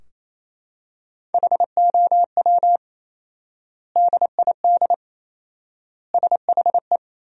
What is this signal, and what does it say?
Morse code, heard as a keyed tone, about 27 wpm, 700 Hz. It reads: HOW DID SHE
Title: 4x Word-Spacing - Sentences from Top 100 Words - 28wpm